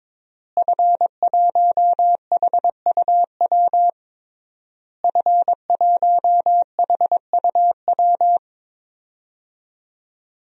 Morse code, heard as a keyed tone, about 22 wpm, 700 Hz. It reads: F1HUW F1HUW
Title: DXCC-100 Call Signs - 5 Characters - Repeated 3x - 22wpm